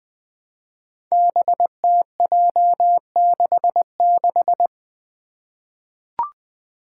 Morse code, heard as a keyed tone, about 20 wpm, 700 Hz. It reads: BTJ66
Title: Five Challenging Letter-Number Combinations 20wpm